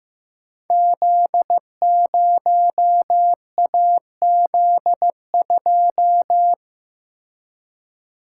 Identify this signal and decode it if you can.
Morse code, heard as a keyed tone, about 15 wpm, 700 Hz. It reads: Z0AZ2
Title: Code Groups - 5 Letter:Numbers - 15wpm